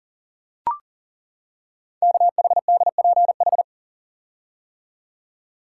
Morse code, heard as a keyed tone, about 39 wpm, 690 Hz. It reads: KHDPH
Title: Code Groups - 5 Letters - 40wpm